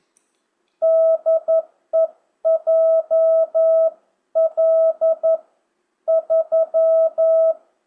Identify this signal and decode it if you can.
Morse code, heard as a keyed tone, about 11 wpm, spaced as for 9 wpm, 660 Hz. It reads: DE JL3
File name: jh8cbh / 2025cw